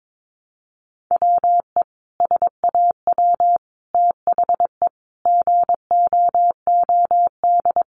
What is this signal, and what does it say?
Morse code, heard as a keyed tone, about 22 wpm, 700 Hz. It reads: WE SAW THE GOOD
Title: Sentences from Top 1100 Words 22wpm